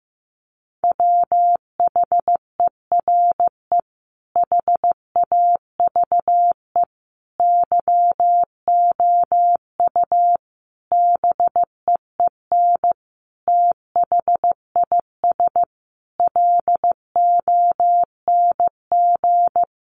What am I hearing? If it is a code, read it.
Morse code, heard as a keyed tone, about 15 wpm, 700 Hz. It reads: WHERE HAVE YOU BEEN THIS LONG
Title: Sentences from Top 300 Words 15wpm